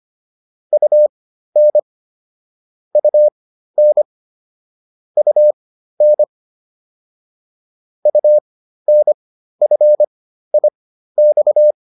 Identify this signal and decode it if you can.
Morse code, heard as a keyed tone, about 25 wpm, spaced as for 13 wpm, 600 Hz. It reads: UN UN UN UNFIX
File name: UN5 13